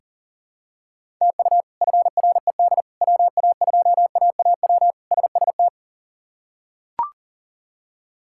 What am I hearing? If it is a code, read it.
Morse code, heard as a keyed tone, about 40 wpm, 700 Hz. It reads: TU FRED WA2AAW SST
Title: K1USN SST QSO Practice - 40wpm